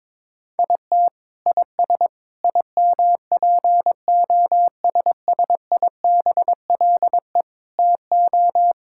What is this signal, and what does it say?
Morse code, heard as a keyed tone, about 22 wpm, 700 Hz. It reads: IT IS IMPOSSIBLE TO
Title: Sentences from Top 1600 Words 22wpm